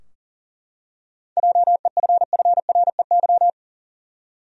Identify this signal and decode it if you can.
Morse code, heard as a keyed tone, about 40 wpm, 700 Hz. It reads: JEFFREY
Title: USA Common Names - 40wpm